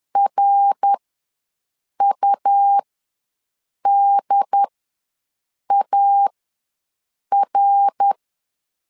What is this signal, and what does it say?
Morse code, heard as a keyed tone, about 11 wpm, spaced as for 6 wpm, 800 Hz. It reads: RUDAR
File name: LEZIONI / 3 MWRO / 3 MWR ws 10wpm 2151